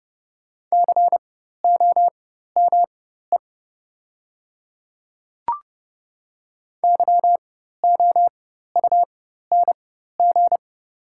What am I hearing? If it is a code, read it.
Morse code, heard as a keyed tone, about 30 wpm, spaced as for 14 wpm, 700 Hz. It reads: COME YOUNG
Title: 4x Character Spacing - Top 500 Words - 30wpm